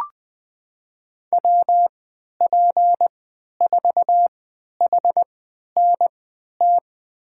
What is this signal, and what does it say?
Morse code, heard as a keyed tone, about 20 wpm, spaced as for 11 wpm, 700 Hz. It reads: WP4HNT
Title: USA Call Signs - 6 Characters - 3x Character Spacing - 20wpm